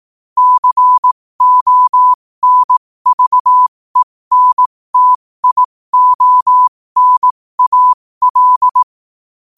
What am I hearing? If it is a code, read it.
Morse code, heard as a keyed tone, about 18 wpm, spaced as for 15 wpm, 1000 Hz. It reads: CONVENTIONAL
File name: qotdfiles / 15wpm